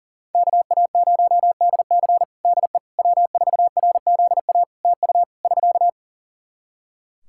Morse code, heard as a keyed tone, about 40 wpm, 700 Hz. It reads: KA0DC DE W4RZA TU <SK>
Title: Practice QSO - Exchange 4 - Ending - 40wpm